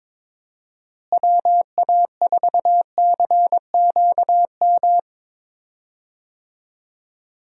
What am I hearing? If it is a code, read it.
Morse code, heard as a keyed tone, about 22 wpm, 700 Hz. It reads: WA4CQM
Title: USA Call Signs - 6 Characters - Repeated 2x - 22wpm